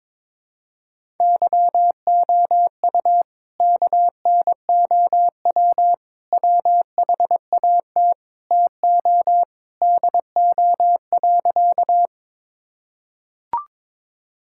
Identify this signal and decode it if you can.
Morse code, heard as a keyed tone, about 22 wpm, 700 Hz. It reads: YOU KNOW WHAT TO DO.
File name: practice / 22wpm / Sentences with 3-Word Phrases - 22wpm